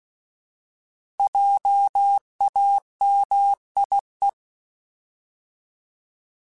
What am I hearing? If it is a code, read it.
Morse code, heard as a keyed tone, about 16 wpm, 800 Hz. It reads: JAMIE
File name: Alphabetical / Alphabetical mp3s / Js @15